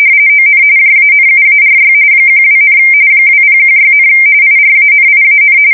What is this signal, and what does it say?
RTTY, about 45.45 baud. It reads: ON JANUARY 12, 2016, IRANIAN
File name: fsk170 / usnavy5